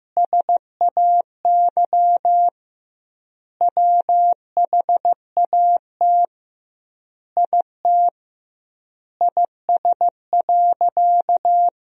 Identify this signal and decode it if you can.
Morse code, heard as a keyed tone, about 15 wpm, 700 Hz. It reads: SAY WHAT IT IS.
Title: 2x Word-Spacing - Sentences from Top 200 Words - 15wpm